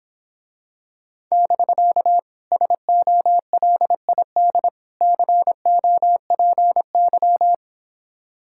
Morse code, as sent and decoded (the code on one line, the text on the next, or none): -...-.- / ... --- .-.. .. -.. / -.-. --- .--. -.--
<BK> SOLID COPY